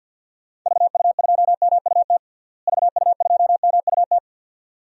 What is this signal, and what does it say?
Morse code, heard as a keyed tone, about 46 wpm, 700 Hz. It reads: VU2MUT VU2MUT
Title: DXCC-100 Call Signs - 6 Characters - Repeated 2x - 50wpm